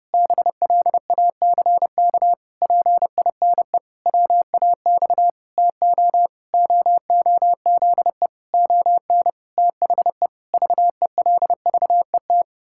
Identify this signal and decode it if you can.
Morse code, heard as a keyed tone, about 32 wpm, 700 Hz. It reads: BLACK PINE WAX TO OOZE ON THE VELVET
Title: Pangram Challenge v2 - 30wpm